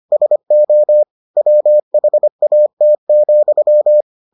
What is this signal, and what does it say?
Morse code, heard as a keyed tone, about 25 wpm, 600 Hz. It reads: SO WHAT,